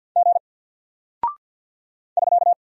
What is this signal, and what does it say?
Morse code, heard as a keyed tone, about 48 wpm, 700 Hz. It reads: G <SK>
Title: Single Character (Letter:Number:Punctuation:Prosign) Alt 50wpm